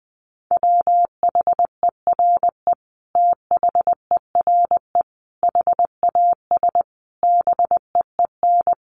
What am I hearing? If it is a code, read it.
Morse code, heard as a keyed tone, about 20 wpm, 700 Hz. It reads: WHERE THERE HAS BEEN
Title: Sentences from Top 1400 Words 20wpm